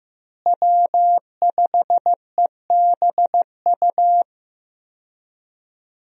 Morse code, heard as a keyed tone, about 15 wpm, 700 Hz. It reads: W5EBU
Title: USA Call Signs - 5 Characters - Repeated 2x - 15wpm